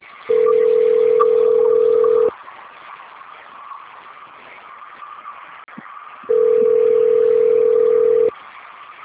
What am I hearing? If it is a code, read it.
Ringback tone (North America), 440 + 480 Hz.